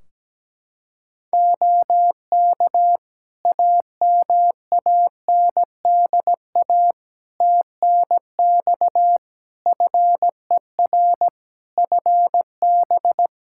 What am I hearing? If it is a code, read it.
Morse code, heard as a keyed tone, about 17 wpm, 700 Hz. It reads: OK AMANDA TNX FER FB
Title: Practice QSO - Exchange 4 - Ending - 17wpm